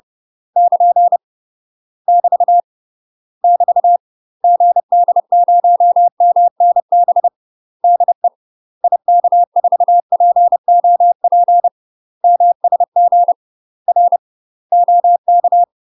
Morse code, heard as a keyed tone, about 30 wpm, 700 Hz. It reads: <KN> <BT> <BT> GD0MNB DE IK4POP MSG R OK
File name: UK MASTER QSOs TRAINING / 30WPM / UK MASTER QSOs 0043